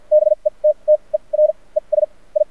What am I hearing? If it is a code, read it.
Morse code, heard as a keyed tone, about 50 wpm, spaced as for 34 wpm, 610 Hz. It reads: BETTER ES I
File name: archived / CRSnet